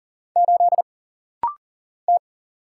Morse code, heard as a keyed tone, about 40 wpm, 700 Hz. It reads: 8 T
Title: Single Letter-Number - Rapid-Fire 40wpm